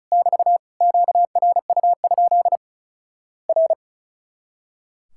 Morse code, heard as a keyed tone, about 36 wpm, 700 Hz. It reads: <BT> QRU? R
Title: Worked All Europe DX Contest - QTC Exchange - 35wpm